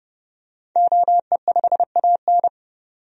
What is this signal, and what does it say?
Morse code, heard as a keyed tone, about 29 wpm, 700 Hz. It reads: OE5AN
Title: DXCC-100 Call Signs - 5 Characters - 30wpm